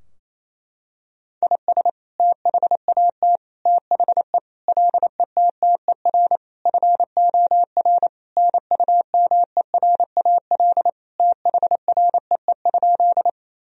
Morse code, heard as a keyed tone, about 28 wpm, 700 Hz. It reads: IS THAT THE LETTER FOR NUMERAL THREE?
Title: Sentences from Top 1000 Words - Review Part 2 - 28wpm